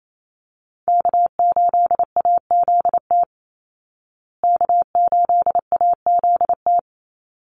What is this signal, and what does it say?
Morse code, heard as a keyed tone, about 28 wpm, 700 Hz. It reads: K8AZT K8AZT
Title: Call Signs Repeated 3x - 28wpm